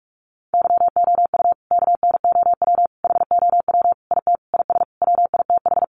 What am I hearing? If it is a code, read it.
Morse code, heard as a keyed tone, about 44 wpm, 700 Hz. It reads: YOU KNOW HOW IT IS WITH
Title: Sentences from Top 100 Words 45wpm